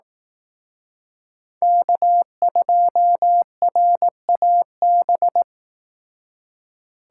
Morse code, heard as a keyed tone, about 18 wpm, 700 Hz. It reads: K2RAB
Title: Speed Racing - USA 1x3 Call Signs 15wpm